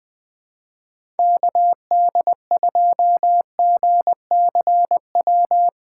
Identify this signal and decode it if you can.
Morse code, heard as a keyed tone, about 21 wpm, 700 Hz. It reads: KD2GCW
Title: USA Call Signs - 6 Characters - 2x Character Spacing - 20wpm